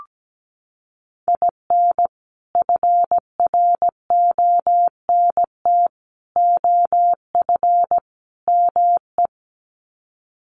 Morse code, heard as a keyed tone, about 17 wpm, 700 Hz. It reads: IN FRONT OF ME
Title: Sets of 4 Words 17wpm